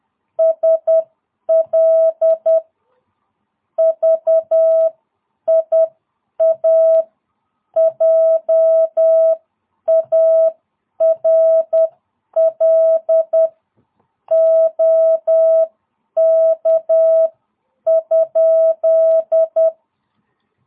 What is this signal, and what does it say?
Morse code, heard as a keyed tone, about 10 wpm, spaced as for 8 wpm, 660 Hz. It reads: SL VIA JARL OK?
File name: jh8cbh / 2022cw